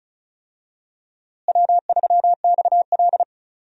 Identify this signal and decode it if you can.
Morse code, heard as a keyed tone, about 36 wpm, 700 Hz. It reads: W3XL
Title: USA Call Signs - 4 Characters - Repeated 3x - 35wpm